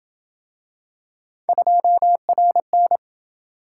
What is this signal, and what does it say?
Morse code, heard as a keyed tone, about 27 wpm, 700 Hz. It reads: 2RN